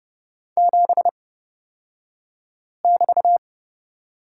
Morse code, heard as a keyed tone, about 29 wpm, spaced as for 5 wpm, 700 Hz. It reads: 7<BT>
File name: Lesson 36d - BT - Two Characters 30wpm